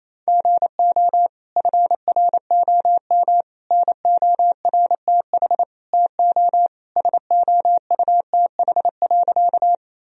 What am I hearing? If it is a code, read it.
Morse code, heard as a keyed tone, about 28 wpm, 700 Hz. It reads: GO FROM NORTH TO SOUTH.
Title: Sentences from Top 1000 Words - Review Part 2 - 28wpm